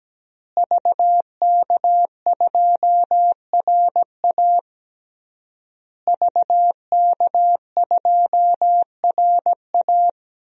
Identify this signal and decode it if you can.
Morse code, heard as a keyed tone, about 17 wpm, 700 Hz. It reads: VK2RA VK2RA
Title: DXCC-100 Call Signs - 5 Characters - Repeated 3x - 17wpm